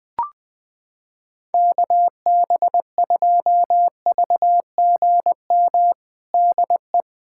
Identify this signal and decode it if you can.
Morse code, heard as a keyed tone, about 21 wpm, 700 Hz. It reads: KB2VGM DE
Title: USA Callsign DE Callsign - 6 Characters - 20wpm